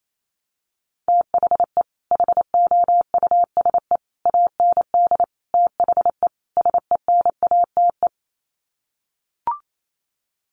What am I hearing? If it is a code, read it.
Morse code, heard as a keyed tone, about 28 wpm, 700 Hz. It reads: THE HOUSE AND THE SENATE
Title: Sets of 5 Words Encore2 28wpm